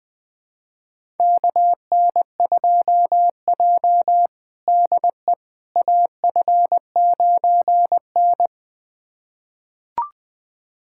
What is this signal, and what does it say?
Morse code, heard as a keyed tone, about 21 wpm, 700 Hz. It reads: KN2J DE AF9N
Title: USA Callsign DE Callsign - 4 Characters - 20wpm